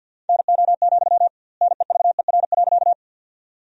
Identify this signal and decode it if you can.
Morse code, heard as a keyed tone, about 50 wpm, 700 Hz. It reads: NO, NEVER.
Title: 2x Word-Spacing - Sentences from Top 300 Words - 50wpm